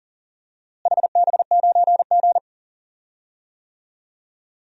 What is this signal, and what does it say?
Morse code, heard as a keyed tone, about 40 wpm, 700 Hz. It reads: HB9G
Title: DXCC-100 Call Signs - 4 Characters - Repeated 2x - 40wpm